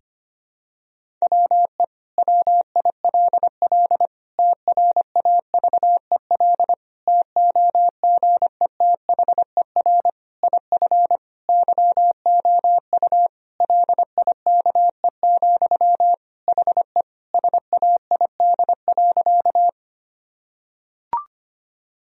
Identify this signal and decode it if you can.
Morse code, heard as a keyed tone, about 25 wpm, 700 Hz. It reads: WE WILL TRAVEL TOGETHER IF YOU LIKE, HE SAID.
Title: Sentences from Top 1000 Words - Review Part 2 - 25wpm